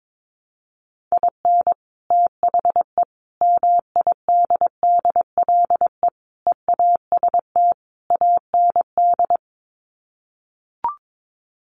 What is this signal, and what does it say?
Morse code, heard as a keyed tone, about 22 wpm, 700 Hz. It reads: IN THE MIDDLE EAST AND
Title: Sets of 5 Words Encore 22wpm